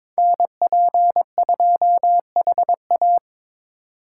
Morse code, heard as a keyed tone, about 26 wpm, 700 Hz. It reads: NP2HA
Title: Speed Racing - USA 2x2 Call Signs 22wpm